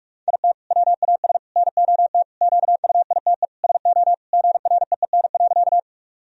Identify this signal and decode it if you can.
Morse code, heard as a keyed tone, about 43 wpm, 700 Hz. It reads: IT WAS NOT QUITE SO GREEN.
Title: Sentences from Top 1000 Words - Review Part 2 - 45wpm